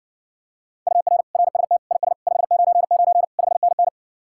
Morse code, heard as a keyed tone, about 60 wpm, 700 Hz. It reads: UR RST IS 599 5NN